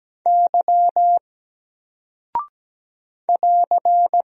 Morse code, heard as a keyed tone, about 17 wpm, 700 Hz. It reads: Y <AR>
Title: Single Character (Letter:Number:Punctuation:Prosign) 17wpm